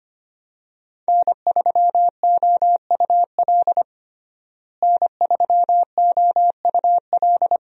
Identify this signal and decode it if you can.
Morse code, heard as a keyed tone, about 25 wpm, 700 Hz. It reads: N3OUL N3OUL
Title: USA Call Signs - 5 Characters - Repeated 3x - 25wpm